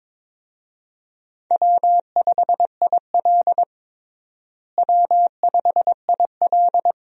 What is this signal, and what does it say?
Morse code, heard as a keyed tone, about 23 wpm, 700 Hz. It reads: W5IL W5IL
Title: USA Call Signs - 4 Characters - Repeated 2x - 22wpm